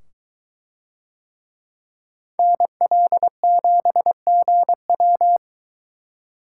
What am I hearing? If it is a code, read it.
Morse code, heard as a keyed tone, about 23 wpm, 700 Hz. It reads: NL7GW